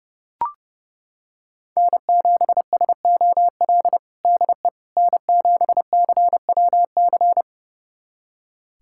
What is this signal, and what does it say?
Morse code, heard as a keyed tone, about 30 wpm, 700 Hz. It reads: N7SOL DE N7CWC
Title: USA Callsign DE Callsign - 5 Characters - 30wpm